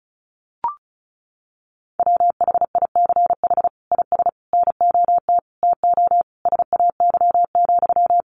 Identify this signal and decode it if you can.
Morse code, heard as a keyed tone, about 34 wpm, 700 Hz. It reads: WHICH IS NOT TO SAY,
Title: Sentences from Top 1400 Words 35wpm